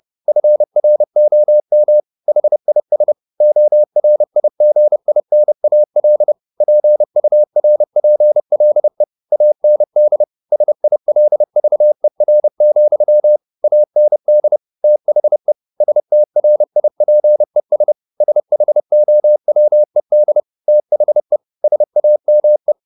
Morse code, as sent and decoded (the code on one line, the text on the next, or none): ..-. .-. --- -- / .... .. ... / --- .-. .. --. .. -. .- .-.. / .--. ..- .-. .--. .-.. / .- -. -.. / ... .. .-.. ...- . .-. --..-- / .- -. -.. / - .... / ... - .-. .. .--. ... / ... .... --- .-- . -.. / - .... / ... .- -- .
FROM HIS ORIGINAL PURPLE AND SILVER, AND THE STRIPES SHOWED THE SAME